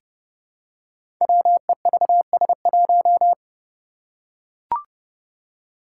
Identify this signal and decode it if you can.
Morse code, heard as a keyed tone, about 30 wpm, 700 Hz. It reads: WEVS1
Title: Five Challenging Letter-Number Combinations 30wpm